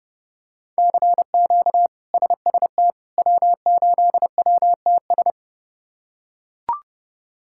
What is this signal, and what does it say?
Morse code, heard as a keyed tone, about 31 wpm, 700 Hz. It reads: CQ SST W8WTS
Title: K1USN SST QSO Practice - 30wpm